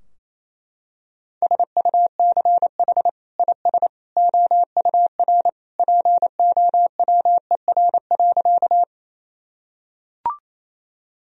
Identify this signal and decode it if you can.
Morse code, heard as a keyed tone, about 28 wpm, 700 Hz. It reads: SUCH IS OUR POWER.
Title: Sentences from Top 1000 Words - Review Part 2 - 28wpm